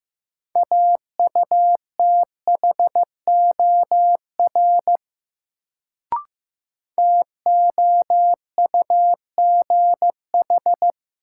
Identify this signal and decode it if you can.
Morse code, heard as a keyed tone, about 15 wpm, 700 Hz. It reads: AUTHOR TOUGH
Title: Sentences from Top 1200 Words - New Word Drill Alt - 15wpm